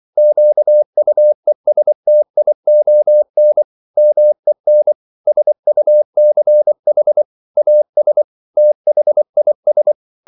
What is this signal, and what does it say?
Morse code, heard as a keyed tone, about 24 wpm, 600 Hz. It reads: QUESTION MEN SUCH AS THIS